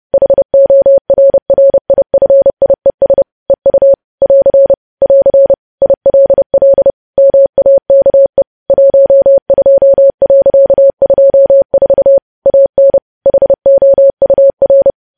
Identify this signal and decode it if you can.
Morse code, heard as a keyed tone, about 30 wpm, 550 Hz. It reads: HORRIFIES EU <AR> <AR> ILL MAKE 12.24 AN HOUR
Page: news headlines in morse code at 30 wpm